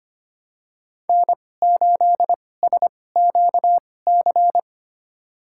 Morse code, heard as a keyed tone, about 25 wpm, spaced as for 18 wpm, 700 Hz. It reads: N8SQC
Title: USA Call Signs - 5 Characters - 2x Character Spacing - 25wpm